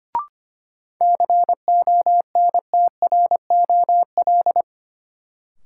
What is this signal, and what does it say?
Morse code, heard as a keyed tone, about 25 wpm, 700 Hz. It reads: CONTROL
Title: Sentences from Top 1000 Words - Review Part 2 - New Word Drill - 25wpm